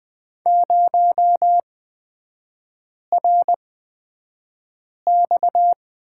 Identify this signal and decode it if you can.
Morse code, heard as a keyed tone, about 20 wpm, spaced as for 5 wpm, 700 Hz. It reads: 0RX
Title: 255 Lesson 41e - Period - Three Characters 20wpm